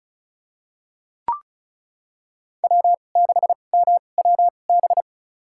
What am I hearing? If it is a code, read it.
Morse code, heard as a keyed tone, about 36 wpm, spaced as for 25 wpm, 700 Hz. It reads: W6MWB
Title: USA Call Signs - 5 Characters - 2x Character Spacing - 35wpm